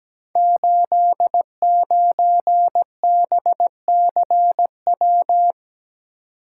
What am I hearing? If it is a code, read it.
Morse code, heard as a keyed tone, about 17 wpm, 700 Hz. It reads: 89BCW
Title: Code Groups - 5 Letter:Numbers - 17wpm